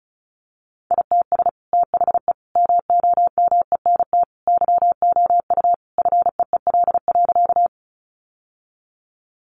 Morse code, heard as a keyed tone, about 34 wpm, 700 Hz. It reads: ITS THE MOMENT YOU FEEL.
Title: Sentences from Top 1000 Words 35wpm